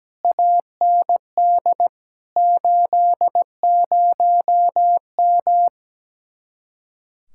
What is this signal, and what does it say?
Morse code, heard as a keyed tone, about 17 wpm, 700 Hz. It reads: AND 80M
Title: Speed-Racing Common QSO Phrases - 17wpm